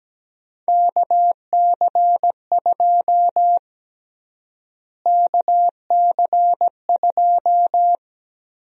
Morse code, heard as a keyed tone, about 17 wpm, 700 Hz. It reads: KC2 KC2
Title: USA Call Sign Prefixes - Repeated 2x - 17wpm